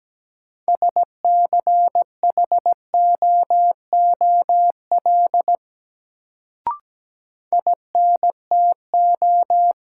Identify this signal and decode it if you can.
Morse code, heard as a keyed tone, about 17 wpm, 700 Hz. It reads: SCHOOL INTO
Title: Sets of 2 Words - New Word Drill - 17wpm